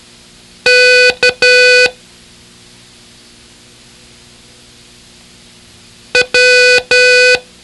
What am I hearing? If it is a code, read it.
Morse code, heard as a keyed tone, about 12 wpm, 500 Hz. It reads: K W
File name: ham radio / CodeQuick / Code Quick 01